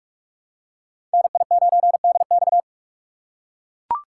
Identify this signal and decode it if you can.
Morse code, heard as a keyed tone, about 42 wpm, 700 Hz. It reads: NI9DX
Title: USA Call Signs - 5 Characters - 2x Character Spacing - 45wpm